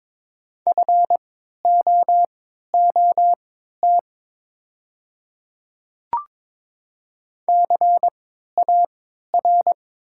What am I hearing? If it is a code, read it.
Morse code, heard as a keyed tone, about 22 wpm, spaced as for 12 wpm, 700 Hz. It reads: FOOT CAR
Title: 3x Character Spacing - Top 500 Words - 22wpm